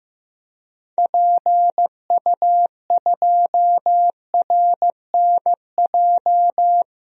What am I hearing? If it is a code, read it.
Morse code, heard as a keyed tone, about 15 wpm, 700 Hz. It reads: PU2RNJ
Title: DXCC-100 Call Signs - 6 Characters - 15wpm